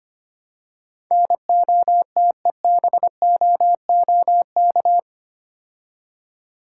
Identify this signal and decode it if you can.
Morse code, heard as a keyed tone, about 29 wpm, 700 Hz. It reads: NOTEBOOK
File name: Easy Words - Speed-Racing with Hints - 25wpm